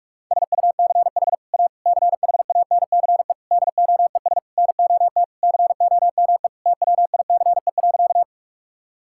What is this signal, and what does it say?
Morse code, heard as a keyed tone, about 45 wpm, 700 Hz. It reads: SUCH A CHANCE DOES NOT COME TWICE.
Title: Sentences from Top 1600 Words 45wpm